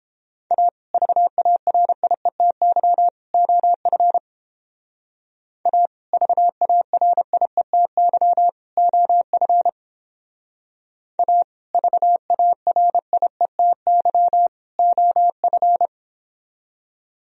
Morse code, heard as a keyed tone, about 29 wpm, 700 Hz. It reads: A VARIETY OF A VARIETY OF A VARIETY OF